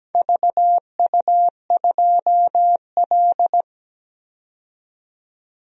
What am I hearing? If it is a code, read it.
Morse code, heard as a keyed tone, about 17 wpm, 700 Hz. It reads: VU2L
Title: DXCC-100 Call Signs - 4 Characters - 17wpm